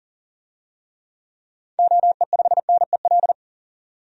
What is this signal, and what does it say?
Morse code, heard as a keyed tone, about 37 wpm, 700 Hz. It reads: OE5NEL
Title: DXCC-100 Call Signs - 6 Characters - 40wpm